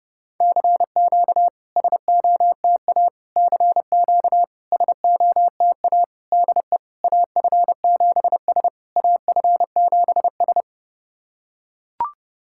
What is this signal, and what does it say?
Morse code, heard as a keyed tone, about 30 wpm, 700 Hz. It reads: CQ SOTA CQ SOTA DE AF7S AF7S
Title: SOTA QSO Practice 30wpm